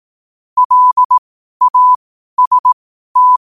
Morse code, heard as a keyed tone, about 18 wpm, spaced as for 12 wpm, 1000 Hz. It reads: LAST
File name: qotdfiles / 12wpm